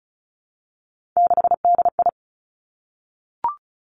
Morse code, heard as a keyed tone, about 36 wpm, 700 Hz. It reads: 6DI
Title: Three Challenging Letter-Number Combinations 35wpm